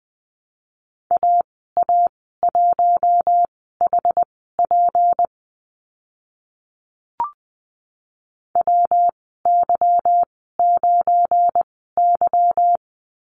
Morse code, heard as a keyed tone, about 20 wpm, spaced as for 14 wpm, 700 Hz. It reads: AA1HP WY9Y
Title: 179 Lesson 29g - Slash - Call Signs 20wpm